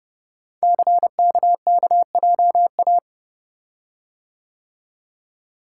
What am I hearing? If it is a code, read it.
Morse code, heard as a keyed tone, about 29 wpm, 700 Hz. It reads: CKKJA
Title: Code Groups - 5 Letters - 30wpm